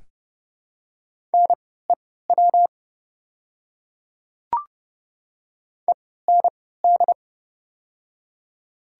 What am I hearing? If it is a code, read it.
Morse code, heard as a keyed tone, about 30 wpm, spaced as for 17 wpm, 700 Hz. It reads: NEW END
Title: 3x Character Spacing - Top 200 Words - 30wpm